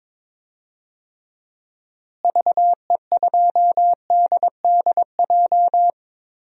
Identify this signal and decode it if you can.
Morse code, heard as a keyed tone, about 21 wpm, 700 Hz. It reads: VE2DDJ
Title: DXCC-100 Call Signs - 6 Characters - Repeated 3x - 22wpm